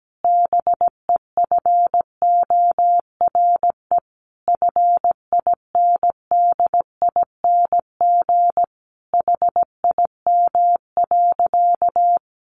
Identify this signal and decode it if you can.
Morse code, heard as a keyed tone, about 17 wpm, 700 Hz. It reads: BEFORE FINDING HIM.
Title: Sentences from Top 1100 Words 17wpm